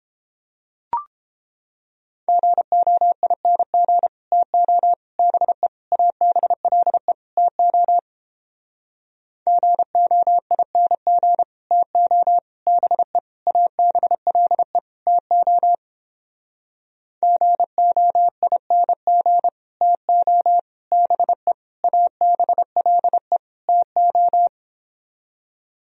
Morse code, heard as a keyed tone, about 29 wpm, 700 Hz. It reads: GOING TO BE ABLE TO GOING TO BE ABLE TO GOING TO BE ABLE TO